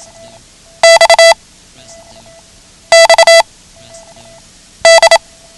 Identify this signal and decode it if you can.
Morse code, heard as a keyed tone, about 27 wpm, 660 Hz. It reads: X X D
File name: ham radio / CodeQuick / Code Quick 09